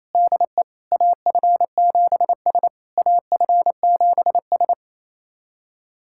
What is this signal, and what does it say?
Morse code, heard as a keyed tone, about 28 wpm, 690 Hz. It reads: DE AF7S AF7S
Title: SOTA QSO Practice 28wpm